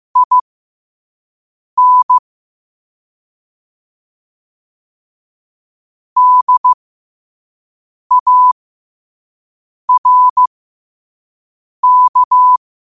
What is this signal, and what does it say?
Morse code, heard as a keyed tone, about 15 wpm, spaced as for 5 wpm, 1000 Hz. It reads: IN DARK
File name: qotdfiles / 5wpm